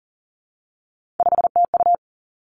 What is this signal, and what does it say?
Morse code, heard as a keyed tone, about 38 wpm, 700 Hz. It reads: 5TU
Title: Three Challenging Letter-Number Combinations 40wpm